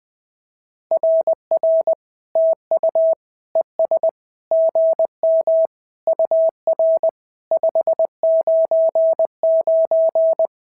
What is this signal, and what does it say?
Morse code, heard as a keyed tone, about 19 wpm, 650 Hz. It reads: RR TU ES GM UR 599
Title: SOTA QSO Practice 20wpm